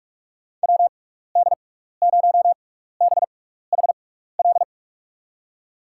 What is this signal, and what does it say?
Morse code, heard as a keyed tone, about 46 wpm, spaced as for 15 wpm, 700 Hz. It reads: WD0BHL
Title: USA Call Signs - 6 Characters - 6x Character Spacing - 45wpm